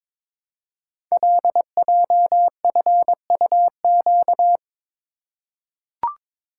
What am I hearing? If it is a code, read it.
Morse code, heard as a keyed tone, about 22 wpm, 700 Hz. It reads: LJFUQ
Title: Code Groups - 5 Letters - 22wpm